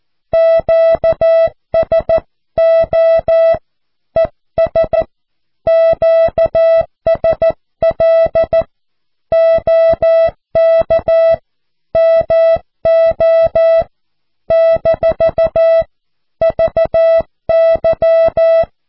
Morse code, heard as a keyed tone, about 14 wpm, 660 Hz. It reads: QSO ES QSL OK MO - VY
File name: cw / QSO